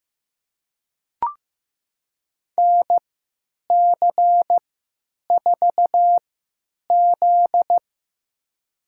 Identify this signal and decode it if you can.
Morse code, heard as a keyed tone, about 15 wpm, spaced as for 9 wpm, 700 Hz. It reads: NC4Z
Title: USA Call Signs - 4 Characters - 3x Character Spacing - 15wpm